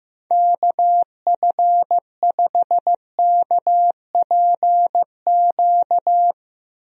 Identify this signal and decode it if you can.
Morse code, heard as a keyed tone, about 15 wpm, 700 Hz. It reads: KF5KPQ
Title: USA Call Signs - 6 Characters - Repeated 2x - 15wpm